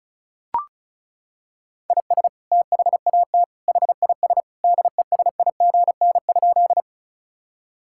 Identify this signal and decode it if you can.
Morse code, heard as a keyed tone, about 36 wpm, 700 Hz. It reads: IS THAT HIS DESIGN?
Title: Sentences from Top 1000 Words - Review Part 5 - 35wpm